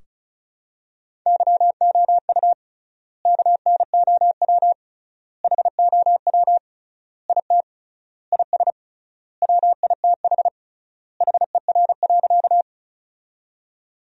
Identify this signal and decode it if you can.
Morse code, heard as a keyed tone, about 36 wpm, 700 Hz. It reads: YOU KNOW HOW IT IS WITH HER.
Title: 3x Word-Spacing - Sentences from Top 100 Words - 35wpm